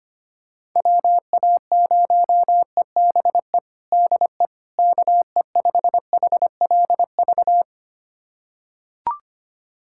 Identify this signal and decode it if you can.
Morse code, heard as a keyed tone, about 25 wpm, 700 Hz. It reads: WA0EBE DE KE5HLV
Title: USA Callsign DE Callsign - 6 Characters - 25wpm